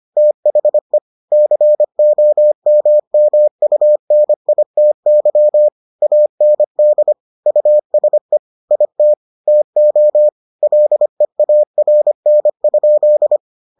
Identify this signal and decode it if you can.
Morse code, heard as a keyed tone, about 25 wpm, 600 Hz. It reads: THE COMMUNITY AND USE IT TO LEARN?